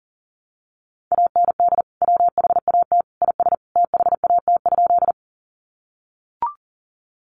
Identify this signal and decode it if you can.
Morse code, heard as a keyed tone, about 40 wpm, 700 Hz. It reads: AND WHAT IS THAT?
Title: Sentences from Top 100 Words 40wpm